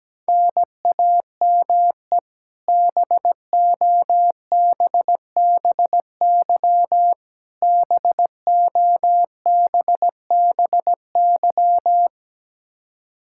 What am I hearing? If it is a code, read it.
Morse code, heard as a keyed tone, about 17 wpm, 700 Hz. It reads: NAME BOBBY BOBBY
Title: Practice QSO - Exchange 1 - Beginning - 17wpm